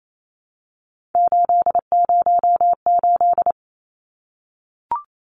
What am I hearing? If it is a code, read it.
Morse code, heard as a keyed tone, about 28 wpm, 700 Hz. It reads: 808
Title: Numbers - 3-Digit 28wpm